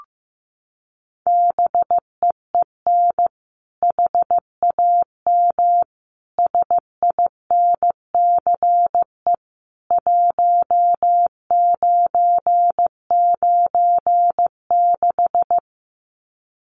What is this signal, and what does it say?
Morse code, heard as a keyed tone, about 15 wpm, 700 Hz. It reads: BEEN HAM SINCE 1996
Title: Common QSO Phrases 15wpm